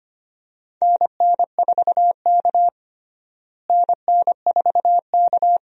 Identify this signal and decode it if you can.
Morse code, heard as a keyed tone, about 25 wpm, 700 Hz. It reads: NN4K NN4K
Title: USA Call Signs - 4 Characters - Repeated 3x - 25wpm